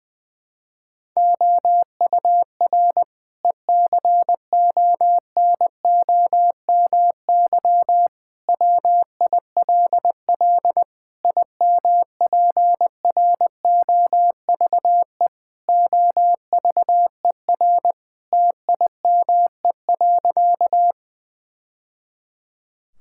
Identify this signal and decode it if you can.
Morse code, heard as a keyed tone, about 20 wpm, 700 Hz. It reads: OUR ECONOMY WILL IMPROVE OVER TIME.
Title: Sentences from Top 1000 Words - Review Part 5 - 20wpm